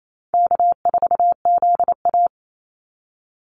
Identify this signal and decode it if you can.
Morse code, heard as a keyed tone, about 28 wpm, 700 Hz. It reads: K4ZA
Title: Easy Call Signs Repeated 2x - 28wpm